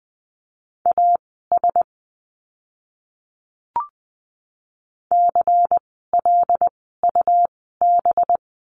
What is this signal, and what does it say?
Morse code, heard as a keyed tone, about 20 wpm, spaced as for 15 wpm, 700 Hz. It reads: AS CLUB
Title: 185 Lesson 30f - Letter b - Words 20wpm